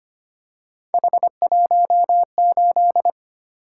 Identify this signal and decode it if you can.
Morse code, heard as a keyed tone, about 25 wpm, 700 Hz. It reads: H18
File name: Three Letter-Number Combinations - 25wpm